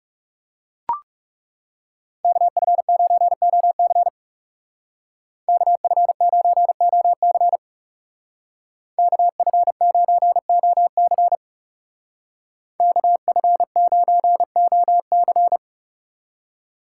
Morse code, heard as a keyed tone, about 36 wpm, 700 Hz. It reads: KF9OC KF9OC KF9OC KF9OC
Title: Speed Racing - USA 2x2 Call Signs 30wpm